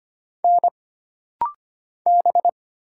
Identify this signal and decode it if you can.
Morse code, heard as a keyed tone, about 25 wpm, 700 Hz. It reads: N B
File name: Single Character (Letter:Number:Punctuation:Prosign) - Rapid-Fire - 25wpm